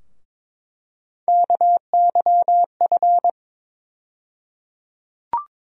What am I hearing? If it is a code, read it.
Morse code, heard as a keyed tone, about 22 wpm, 700 Hz. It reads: KYF
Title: USA Call Sign Postfixes - 3 Letters 22wpm